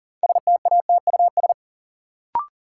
Morse code, heard as a keyed tone, about 40 wpm, 700 Hz. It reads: STATUS
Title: Sentences from Top 1400 Words - New Word Drill Alt - 40wpm